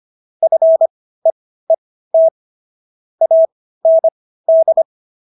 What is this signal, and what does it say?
Morse code, heard as a keyed tone, about 25 wpm, spaced as for 15 wpm, 650 Hz. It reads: FEET AND